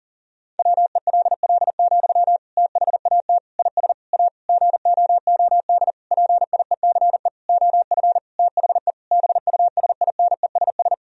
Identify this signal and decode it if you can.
Morse code, heard as a keyed tone, about 37 wpm, 700 Hz. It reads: WELL, THAT IS A GOOD PIECE OF THE BUSINESS
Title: Sentences from Top 1000 Words - Review Part 2 - 40wpm